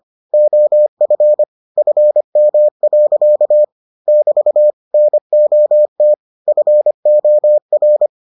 Morse code, heard as a keyed tone, about 25 wpm, 600 Hz. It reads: OF FM. <BT> NOT FOR